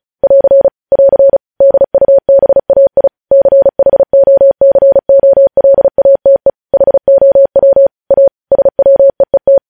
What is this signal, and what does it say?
Morse code, heard as a keyed tone, about 36 wpm, 550 Hz. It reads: <AR> <AR> DUBAI CHOCOLATE HOW A SWEET
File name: podcast / News headlines-1x-35x35